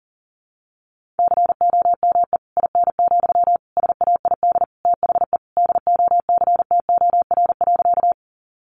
Morse code, heard as a keyed tone, about 40 wpm, 700 Hz. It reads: COME IN, SAID THE DOCTOR.
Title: Sentences from Top 1500 Words 40wpm